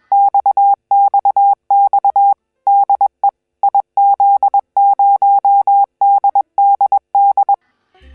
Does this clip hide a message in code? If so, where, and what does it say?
Yes: Morse code from 0.1 to 7.5 s, heard as a keyed tone, about 21 wpm, 800 Hz. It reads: XXX DE IZ0DDD